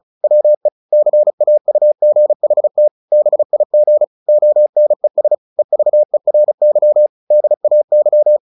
Morse code, heard as a keyed tone, about 34 wpm, 600 Hz. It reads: WE CAUGHT BIG ONES EVERY DAY
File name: the old man and the sea / the old man and the sea 1 35wpm